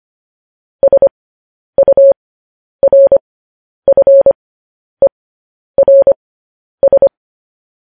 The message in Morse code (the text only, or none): SURFERS